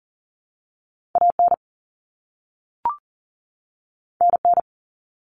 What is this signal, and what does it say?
Morse code, heard as a keyed tone, about 40 wpm, 700 Hz. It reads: AN NN
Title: Common Cut Numbers - 2-Digit 40wpm